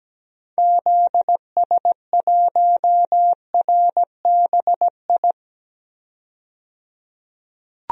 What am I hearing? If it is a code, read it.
Morse code, heard as a keyed tone, about 17 wpm, 700 Hz. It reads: ZS1RBI
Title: DXCC-100 Call Signs - 6 Characters - 17wpm